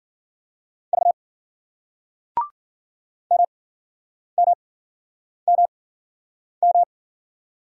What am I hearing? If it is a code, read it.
Morse code, heard as a keyed tone, about 48 wpm, spaced as for 9 wpm, 700 Hz. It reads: V MMMM
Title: Speed-Racing Single Letters 40wpm